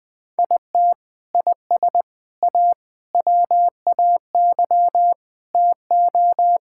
Morse code, heard as a keyed tone, about 21 wpm, 700 Hz. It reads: IT IS A WAY TO
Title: Sentences with 3-Word Phrases - Spoken Beginning - 20wpm